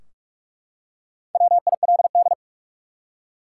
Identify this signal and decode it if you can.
Morse code, heard as a keyed tone, about 42 wpm, 700 Hz. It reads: WILD
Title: Four Letter Words - 45wpm